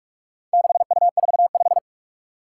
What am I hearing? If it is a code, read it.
Morse code, heard as a keyed tone, about 41 wpm, 700 Hz. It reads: 6U45
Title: Four Challenging Letter-Number Combinations 45wpm